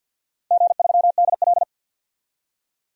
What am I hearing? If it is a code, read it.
Morse code, heard as a keyed tone, about 47 wpm, 700 Hz. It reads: G3DL
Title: DXCC-100 Call Signs - 4 Characters - Repeated 3x - 50wpm